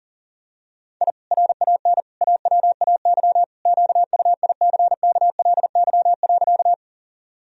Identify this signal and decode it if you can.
Morse code, heard as a keyed tone, about 40 wpm, 700 Hz. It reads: I RAN AWAY QUICKLY.
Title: Sentences from Top 1000 Words - Review Part 3 - 40wpm